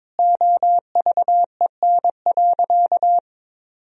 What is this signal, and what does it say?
Morse code, heard as a keyed tone, about 22 wpm, 700 Hz. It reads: OVEN.
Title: Sentences from Top 1000 Words - Review Part 4 - 22wpm